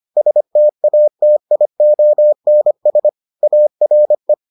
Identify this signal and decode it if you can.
Morse code, heard as a keyed tone, about 25 wpm, 600 Hz. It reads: STATIONS ARE